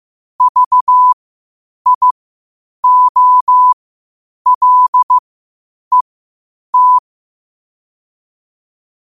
Morse code, heard as a keyed tone, about 15 wpm, spaced as for 8 wpm, 1000 Hz. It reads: VIOLET
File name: qotdfiles / 8wpm